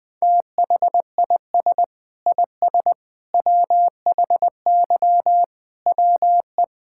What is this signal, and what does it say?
Morse code, heard as a keyed tone, about 20 wpm, 700 Hz. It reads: THIS IS WHY WE
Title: Sets of 8 Words - 20wpm